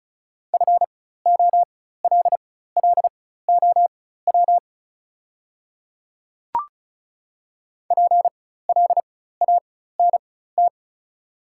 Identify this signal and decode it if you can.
Morse code, heard as a keyed tone, about 36 wpm, spaced as for 16 wpm, 700 Hz. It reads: FOLLOW PLANT
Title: 4x Character Spacing - Top 300 Words - 35wpm